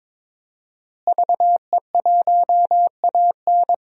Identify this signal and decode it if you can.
Morse code, heard as a keyed tone, about 22 wpm, 700 Hz. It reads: VE1AN
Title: DXCC-100 Call Signs - 5 Characters - Repeated 2x - 22wpm